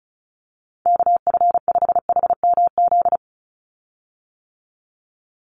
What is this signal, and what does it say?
Morse code, heard as a keyed tone, about 34 wpm, 700 Hz. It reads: KF5HMZ
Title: USA Call Signs 35wpm